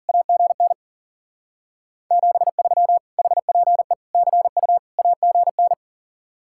Morse code, heard as a keyed tone, about 51 wpm, 700 Hz. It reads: AGN 73 HPE CU AGN